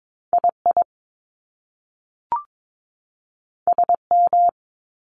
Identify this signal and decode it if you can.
Morse code, heard as a keyed tone, about 22 wpm, 700 Hz. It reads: II SM
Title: Top Two Letter Combinations in Words 22wpm